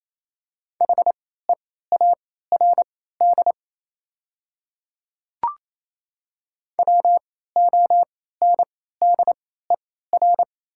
Morse code, heard as a keyed tone, about 28 wpm, spaced as for 16 wpm, 700 Hz. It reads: HEARD WONDER
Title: 3x Character Spacing - Top 500 Words - 28wpm